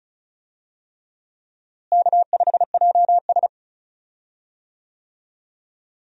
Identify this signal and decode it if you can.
Morse code, heard as a keyed tone, about 36 wpm, 700 Hz. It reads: K5JS
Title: USA Call Signs - 4 Characters - Repeated 3x - 35wpm